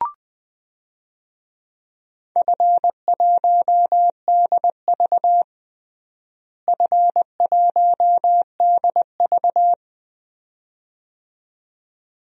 Morse code, heard as a keyed tone, about 21 wpm, 700 Hz. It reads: F1DV F1DV
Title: DXCC-100 Call Signs - 4 Characters - Repeated 2x - 20wpm